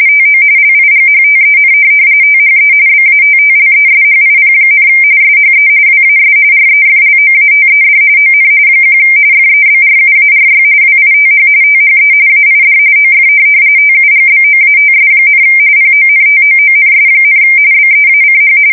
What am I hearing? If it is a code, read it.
RTTY, about 45.45 baud. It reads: TO LIGHTEN THE BATTLESHIP, SHE WAS REFLOATED ON 1 FEBRUARY WITH THE AID OF TUGBOATS, PONTOONS, BEACH
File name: fsk170 / USS-Missouri